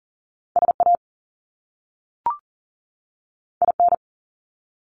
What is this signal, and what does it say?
Morse code, heard as a keyed tone, about 42 wpm, 700 Hz. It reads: SA IN
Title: Top Two Letter Combinations in Words 40wpm